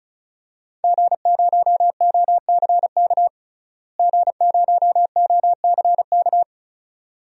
Morse code, heard as a keyed tone, about 36 wpm, 700 Hz. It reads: G0OCK G0OCK
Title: DXCC-100 Call Signs - 5 Characters - Repeated 2x - 35wpm